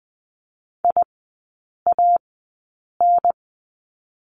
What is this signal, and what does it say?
Morse code, heard as a keyed tone, about 20 wpm, spaced as for 8 wpm, 700 Hz. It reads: IAN
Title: 127 Lesson 21e - Number 6 - Three Characters 20wpm